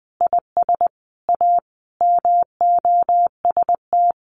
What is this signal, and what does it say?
Morse code, heard as a keyed tone, about 20 wpm, 700 Hz. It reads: IS A MOST
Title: Sentences from Top 1000 Words 20wpm